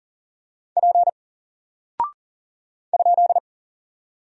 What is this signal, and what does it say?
Morse code, heard as a keyed tone, about 40 wpm, 700 Hz. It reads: P ?
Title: Single Character (Letter:Number:Punctuation:Prosign) 40wpm